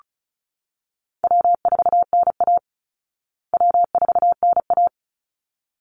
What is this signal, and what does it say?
Morse code, heard as a keyed tone, about 34 wpm, 700 Hz. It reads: W4NA W4NA
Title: Call Signs Repeated 3x - 35wpm